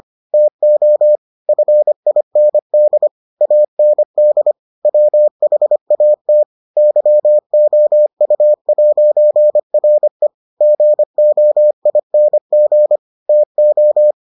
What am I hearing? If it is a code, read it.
Morse code, heard as a keyed tone, about 25 wpm, 600 Hz. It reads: TO FIND AND WHAT YOU'RE GOING TO